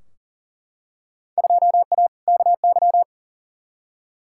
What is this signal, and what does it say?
Morse code, heard as a keyed tone, about 40 wpm, 700 Hz. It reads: 2A KY
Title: Field Day Exchanges - 40wpm